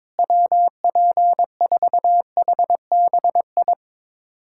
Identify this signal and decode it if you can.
Morse code, heard as a keyed tone, about 22 wpm, 700 Hz. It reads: WP4HBI
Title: USA Callsign DE Callsign 22wpm